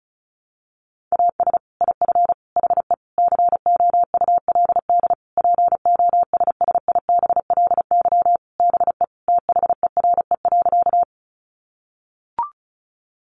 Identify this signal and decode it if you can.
Morse code, heard as a keyed tone, about 34 wpm, 700 Hz. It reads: AS IF HE COULD POSSIBLY BE THERE.
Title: Sentences from Top 1300 Words 35wpm